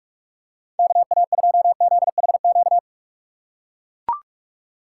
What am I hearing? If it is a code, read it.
Morse code, heard as a keyed tone, about 44 wpm, 700 Hz. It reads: KA2ZHQ
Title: USA Call Signs - 2x Character Spacing - 45wpm